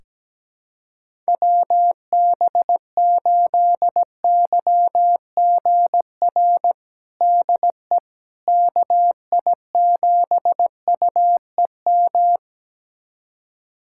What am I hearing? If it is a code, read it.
Morse code, heard as a keyed tone, about 17 wpm, 700 Hz. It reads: WB8YGR DE KI7UEM
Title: USA Callsign DE Callsign - 6 Characters - 17wpm